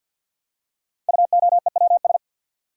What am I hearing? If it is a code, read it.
Morse code, heard as a keyed tone, about 50 wpm, 700 Hz. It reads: UOEWS
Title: Code Groups - 5 Letters - 50wpm